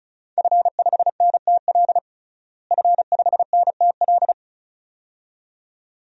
Morse code, heard as a keyed tone, about 36 wpm, 700 Hz. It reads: F5NTL F5NTL
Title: DXCC-100 Call Signs - 5 Characters - Repeated 2x - 35wpm